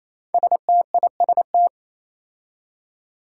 Morse code, heard as a keyed tone, about 28 wpm, 700 Hz. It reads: STIST
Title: Five Challenging Letter-Number Combinations 28wpm